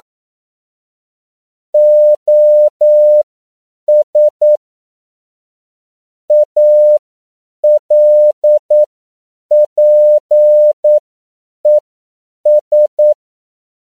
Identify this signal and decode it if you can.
Morse code, heard as a keyed tone, about 9 wpm, spaced as for 7 wpm, 600 Hz. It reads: OS ALPES